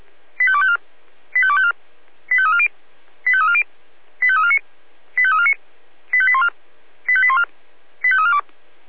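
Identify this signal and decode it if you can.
Five-tone selective call (ZVEI), heard as quick runs of five tones, tones of about 70 ms. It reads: 85254 85254 85340 85340 85349 85349 85814 85814 85342
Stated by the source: ZVEI five-tone selective call 85888 received by SDR.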